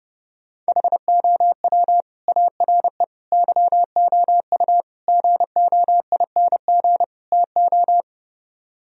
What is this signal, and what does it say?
Morse code, heard as a keyed tone, about 32 wpm, 700 Hz. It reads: HOW ARE YOU GOING TO